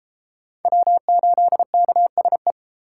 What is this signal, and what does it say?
Morse code, heard as a keyed tone, about 29 wpm, 700 Hz. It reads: W8KSE